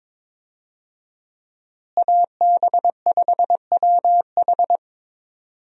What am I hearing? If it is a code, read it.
Morse code, heard as a keyed tone, about 22 wpm, 700 Hz. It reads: AB5WH
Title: USA Call Signs - 5 Characters - Repeated 2x - 22wpm